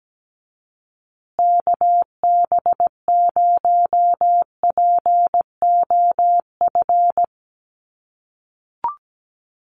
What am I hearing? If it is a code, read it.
Morse code, heard as a keyed tone, about 17 wpm, 700 Hz. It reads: KB0POF
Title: USA Call Signs Repeated 2x 17wpm